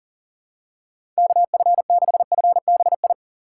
Morse code, heard as a keyed tone, about 40 wpm, 700 Hz. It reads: KF6FBI
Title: USA Call Signs - 6 Characters - 2x Character Spacing - 40wpm